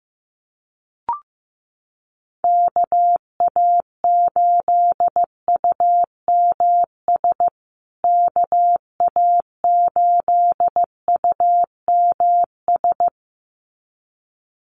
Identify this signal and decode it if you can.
Morse code, heard as a keyed tone, about 15 wpm, 700 Hz. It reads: KA8UMS KA8UMS
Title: USA Call Signs Repeated 2x 15wpm